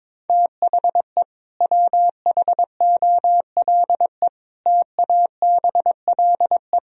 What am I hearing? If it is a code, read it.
Morse code, heard as a keyed tone, about 21 wpm, 700 Hz. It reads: THE WHOLE TABLE
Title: Sentences from Top 1000 Words - Review Part 2 - 22wpm